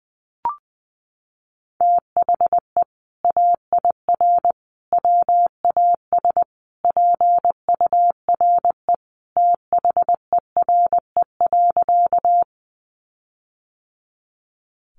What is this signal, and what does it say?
Morse code, heard as a keyed tone, about 20 wpm, 700 Hz. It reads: THE AIR WAS PURE THERE.
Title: Sentences from Top 1500 Words 20wpm